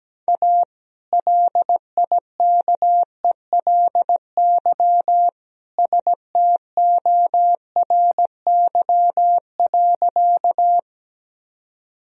Morse code, heard as a keyed tone, about 17 wpm, 700 Hz. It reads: A LIKELY STORY.
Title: Sentences from Top 1000 Words - Review Part 4 - 17wpm